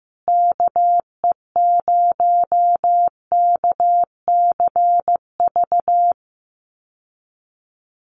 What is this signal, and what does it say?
Morse code, heard as a keyed tone, about 14 wpm, 700 Hz. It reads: KE0KCV
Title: USA Call Signs Repeated 3x 15wpm